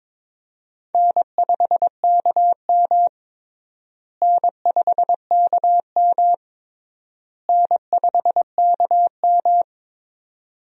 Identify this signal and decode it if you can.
Morse code, heard as a keyed tone, about 22 wpm, 700 Hz. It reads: N5KM N5KM N5KM
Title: USA Call Signs - 4 Characters - Repeated 3x - 22wpm